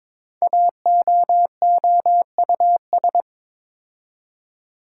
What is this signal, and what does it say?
Morse code, heard as a keyed tone, about 22 wpm, 700 Hz. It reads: AOOUS
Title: Code Groups - 5 Letters - 22wpm